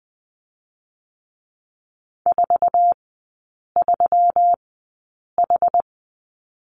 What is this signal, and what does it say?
Morse code, heard as a keyed tone, about 20 wpm, spaced as for 8 wpm, 700 Hz. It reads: 43H
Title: 184 Lesson 30e - Letter b - Three Characters 20wpm